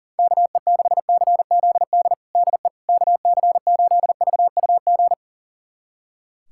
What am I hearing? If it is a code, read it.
Morse code, heard as a keyed tone, about 40 wpm, 700 Hz. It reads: KE6CZD DE KC8VUG
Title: USA Callsign DE Callsign - 6 Characters - 40wpm